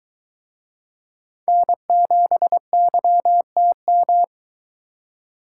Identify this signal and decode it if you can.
Morse code, heard as a keyed tone, about 23 wpm, 700 Hz. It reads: N7YTM